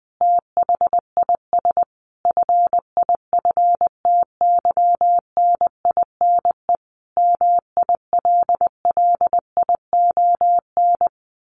Morse code, heard as a keyed tone, about 20 wpm, 700 Hz. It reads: THIS FIFTYNINE MILLION
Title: A Sound Of Thunder 20wpm